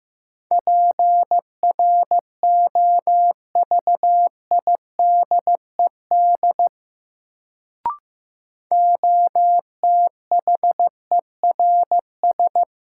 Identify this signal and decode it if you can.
Morse code, heard as a keyed tone, about 15 wpm, 700 Hz. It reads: PROVIDED OTHERS
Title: Sentences from Top 1500 Words - New Word Drill Alt - 15wpm